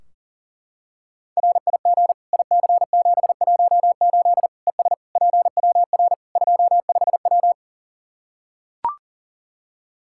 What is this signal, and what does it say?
Morse code, heard as a keyed tone, about 40 wpm, 700 Hz. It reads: RIG IC718 ES PWR 25W
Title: Practice QSO - Exchange 2 - Info - 40wpm